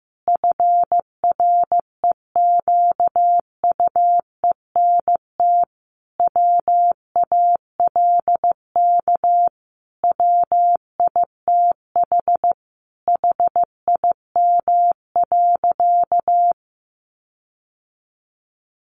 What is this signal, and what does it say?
Morse code, heard as a keyed tone, about 15 wpm, 700 Hz. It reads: FREQUENT WALK WITH HIM.
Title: Sentences from Top 1400 Words 15wpm